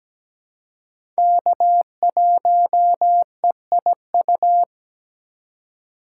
Morse code, heard as a keyed tone, about 17 wpm, 700 Hz. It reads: K1EIU
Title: USA Call Signs - 5 Characters - 4x Character Spacing - 17wpm